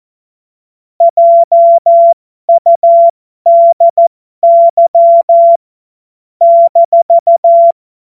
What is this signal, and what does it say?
Morse code, heard as a keyed tone, about 14 wpm, spaced as for 12 wpm, 680 Hz. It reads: JUDY -
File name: Morsekurs / Witze